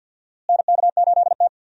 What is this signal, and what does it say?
Morse code, heard as a keyed tone, about 53 wpm, 700 Hz. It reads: NK8T
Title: USA Call Signs - 4 Characters - Repeated 3x - 50wpm